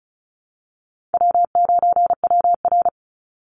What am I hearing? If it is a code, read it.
Morse code, heard as a keyed tone, about 34 wpm, 700 Hz. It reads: W9WR
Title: Easy Call Signs Repeated 3x - 35wpm